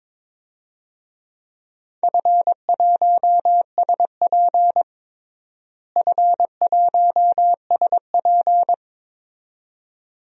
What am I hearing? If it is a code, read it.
Morse code, heard as a keyed tone, about 22 wpm, 700 Hz. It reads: F1SP F1SP
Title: DXCC-100 Call Signs - 4 Characters - Repeated 2x - 22wpm